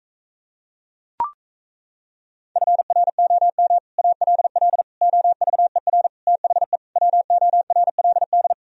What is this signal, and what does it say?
Morse code, heard as a keyed tone, about 38 wpm, 700 Hz. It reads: FROM ALL OVER THE WORLD